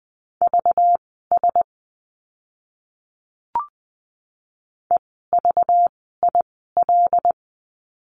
Morse code, heard as a keyed tone, about 20 wpm, spaced as for 15 wpm, 700 Hz. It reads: VS EVIL
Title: 192 Lesson 31f - Letter v - Words 20wpm